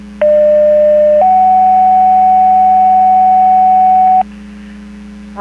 Two-tone page, heard as a one-second tone, then a three-second tone, about 600 then 750 Hz. It is heard from 0.2 to 4.2 s.
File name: Portage 154.385 Paging